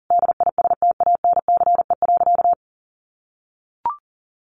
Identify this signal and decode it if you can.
Morse code, heard as a keyed tone, about 40 wpm, 700 Hz. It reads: DISTANCE.
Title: Sentences from Top 1500 Words 40wpm